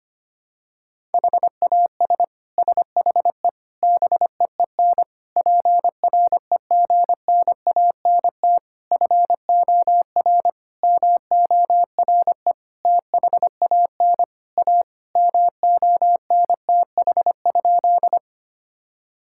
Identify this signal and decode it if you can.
Morse code, heard as a keyed tone, about 25 wpm, 700 Hz. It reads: HAS SHE BEEN PREGNANT FOR MORE THAN A MONTH?
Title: Sentences from Top 1000 Words - Review Part 6 - 25wpm